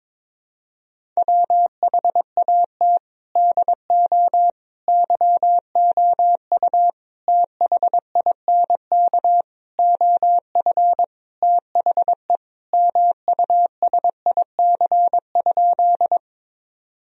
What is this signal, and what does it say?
Morse code, heard as a keyed tone, about 22 wpm, 700 Hz. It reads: WHAT DO YOU THINK OF THE MUSIC?
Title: Sentences from Top 1000 Words - Review Part 1 - 22wpm